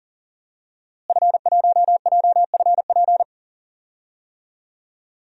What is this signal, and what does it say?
Morse code, heard as a keyed tone, about 40 wpm, 700 Hz. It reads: F1JFP
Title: DXCC-100 Call Signs - 5 Characters - Repeated 2x - 40wpm